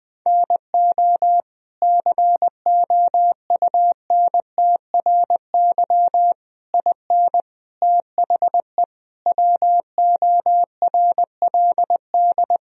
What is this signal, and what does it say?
Morse code, heard as a keyed tone, about 21 wpm, 700 Hz. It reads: NO COUNTRY IN THE WORLD
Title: Sentences from Top 1000 Words - Review Part 3 - 20wpm